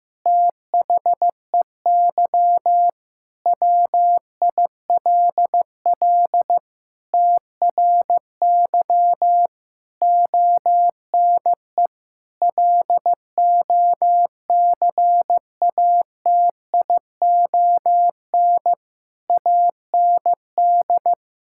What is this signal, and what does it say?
Morse code, heard as a keyed tone, about 15 wpm, 700 Hz. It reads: THEY WILL TRY ONE LOCATION AND
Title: Sentences from Top 1000 Words - Review Part 6 - 15wpm